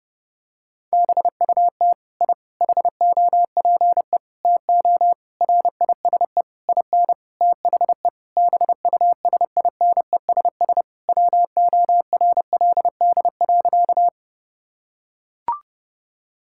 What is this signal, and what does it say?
Morse code, heard as a keyed tone, about 30 wpm, 700 Hz. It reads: BUT I HOPE TO RISE IN THE BUSINESS WORLD.
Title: Sentences from Top 1000 Words - Review Part 6 - 30wpm